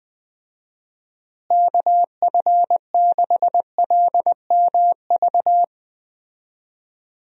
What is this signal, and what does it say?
Morse code, heard as a keyed tone, about 20 wpm, 700 Hz. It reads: KF6LMV
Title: USA Call Signs - 6 Characters - Repeated 3x - 20wpm